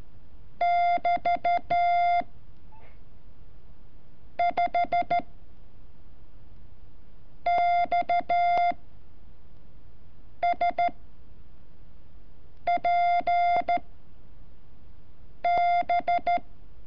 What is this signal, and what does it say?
Morse code, heard as a keyed tone, about 13 wpm, 700 Hz. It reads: <BT> 5 X S P B
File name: curso / ejercicio 30